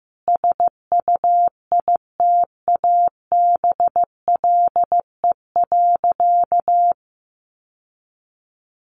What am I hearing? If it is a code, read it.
Morse code, heard as a keyed tone, about 14 wpm, 700 Hz. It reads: SUITABLE.
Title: Sentences from Top 1400 Words 15wpm